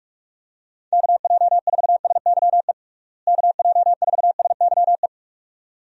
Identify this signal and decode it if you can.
Morse code, heard as a keyed tone, about 41 wpm, 700 Hz. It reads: KJ4SYE KJ4SYE
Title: USA Call Signs - 6 Characters - Repeated 2x - 45wpm